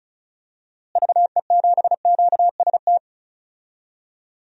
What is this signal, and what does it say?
Morse code, heard as a keyed tone, about 36 wpm, 700 Hz. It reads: VE7QST
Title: DXCC-100 Call Signs - 6 Characters - Repeated 3x - 35wpm